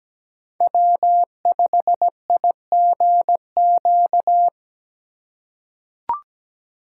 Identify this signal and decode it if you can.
Morse code, heard as a keyed tone, about 17 wpm, 700 Hz. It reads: W5IGQ
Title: USA Call Signs - 5 Characters - 3x Character Spacing - 17wpm